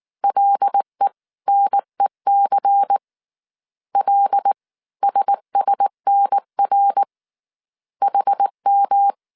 Morse code, heard as a keyed tone, about 19 wpm, 800 Hz. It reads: LENEC L SSNR HM
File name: CW / DET / 1 6 1500C S C 18wpm 2356